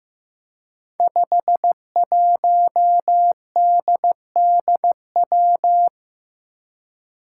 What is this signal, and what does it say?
Morse code, heard as a keyed tone, about 15 wpm, 700 Hz. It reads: 51DDW
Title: Code Groups - 5 Letter:Numbers - 15wpm